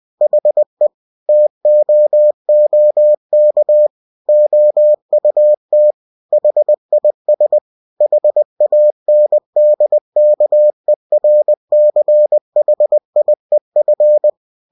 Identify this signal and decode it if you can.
Morse code, heard as a keyed tone, about 20 wpm, 600 Hz. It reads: HE TOOK OUT HIS HANDKERCHIEF
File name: cw-books / winnie-the-pooh / winnie-20wpm-page-0088